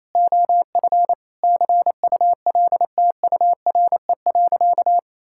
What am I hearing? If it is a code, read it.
Morse code, heard as a keyed tone, about 28 wpm, 700 Hz. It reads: OF CULTURE.
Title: Sentences from Top 1000 Words - Review Part 6 - 28wpm